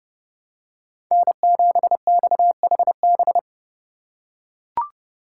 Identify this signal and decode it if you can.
Morse code, heard as a keyed tone, about 29 wpm, 700 Hz. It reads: N7XHB